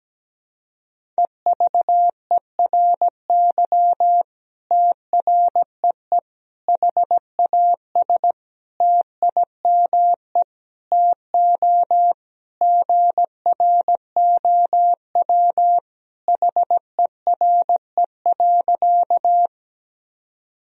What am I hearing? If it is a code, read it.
Morse code, heard as a keyed tone, about 17 wpm, 700 Hz. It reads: EVERY TREE HAS TIME TO GROW HERE.
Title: Sentences from Top 1000 Words - Review Part 1 - 17wpm